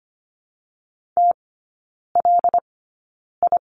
Morse code, heard as a keyed tone, about 25 wpm, spaced as for 9 wpm, 700 Hz. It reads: TLI
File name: Lesson 15e - Number 5 - Three Characters 25wpm